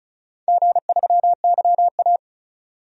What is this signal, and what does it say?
Morse code, heard as a keyed tone, about 36 wpm, 690 Hz. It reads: G3YA
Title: DXCC-100 Call Signs - 4 Characters - 35wpm